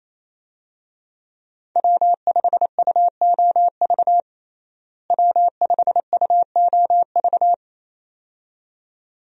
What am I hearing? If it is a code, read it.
Morse code, heard as a keyed tone, about 28 wpm, 700 Hz. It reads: W5UOV W5UOV
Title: USA Call Signs - 5 Characters - Repeated 2x - 28wpm